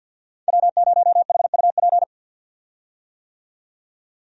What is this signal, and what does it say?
Morse code, heard as a keyed tone, about 52 wpm, 700 Hz. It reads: W0HUP
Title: USA Call Signs - 5 Characters - 50wpm